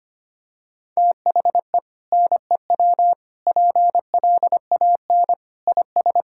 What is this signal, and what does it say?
Morse code, heard as a keyed tone, about 25 wpm, 700 Hz. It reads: THE NEW PLAN IS
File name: practice / 25wpm / Sentences with 2-Word Phrases - 25wpm